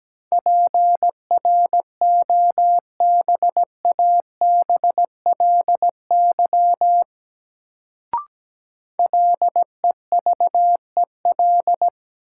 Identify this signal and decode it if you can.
Morse code, heard as a keyed tone, about 17 wpm, 700 Hz. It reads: PROBABLY LEVEL
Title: Sentences from Top 700 Words - New Word Drill - 17wpm